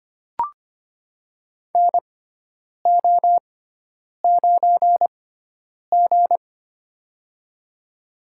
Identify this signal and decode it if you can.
Morse code, heard as a keyed tone, about 25 wpm, spaced as for 9 wpm, 700 Hz. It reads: NO9G
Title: USA Call Signs - 4 Characters - 6x Character Spacing - 25wpm